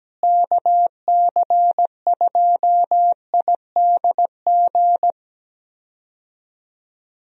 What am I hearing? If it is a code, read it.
Morse code, heard as a keyed tone, about 17 wpm, 700 Hz. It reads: KC2IDG
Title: USA Call Signs - 6 Characters - Repeated 2x - 17wpm